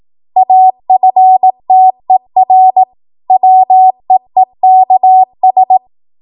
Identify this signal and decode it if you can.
Morse code, heard as a keyed tone, about 18 wpm, 750 Hz. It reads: AFTER WEEKS
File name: Morse / 18WPM